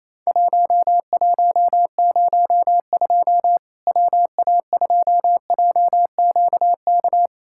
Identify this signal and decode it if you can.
Morse code, heard as a keyed tone, about 28 wpm, 700 Hz. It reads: 1102 WA2JQK
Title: Worked All Europe DX Contest - QTC Exchange - 28wpm